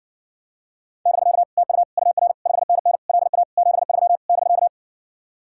Morse code, heard as a keyed tone, about 70 wpm, 700 Hz. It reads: <BK> TU UR 5NN LA 73 <BK>